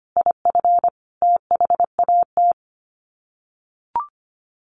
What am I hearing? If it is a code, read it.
Morse code, heard as a keyed tone, about 25 wpm, 700 Hz. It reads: IF THAT
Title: Sets of 5 Words Encore2 25wpm